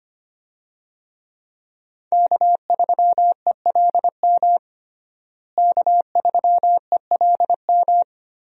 Morse code, heard as a keyed tone, about 25 wpm, 700 Hz. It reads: K3ELM K3ELM
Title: USA Call Signs - 5 Characters - Repeated 2x - 25wpm